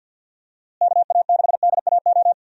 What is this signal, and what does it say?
Morse code, heard as a keyed tone, about 52 wpm, 700 Hz. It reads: KA6DAO
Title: USA Call Signs - 6 Characters - Repeated 2x - 50wpm